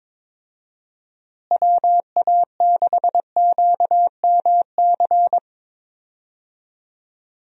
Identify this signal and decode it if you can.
Morse code, heard as a keyed tone, about 21 wpm, 700 Hz. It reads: WA6QMC
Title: USA Call Signs - 6 Characters - Repeated 2x - 22wpm